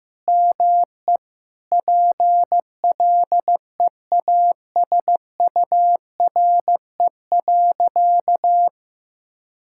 Morse code, heard as a keyed tone, about 15 wpm, 700 Hz. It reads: ME PLEASURE.
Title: Sentences from Top 1600 Words 15wpm